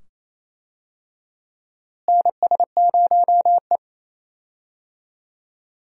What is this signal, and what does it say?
Morse code, heard as a keyed tone, about 28 wpm, 700 Hz. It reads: NS0E
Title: USA Call Signs - 4 Characters - Repeated 2x - 28wpm